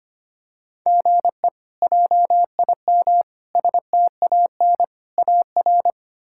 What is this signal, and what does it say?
Morse code, heard as a keyed tone, about 25 wpm, 700 Hz. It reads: GE JIM STAN AR
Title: K1USN SST QSO Practice - 25wpm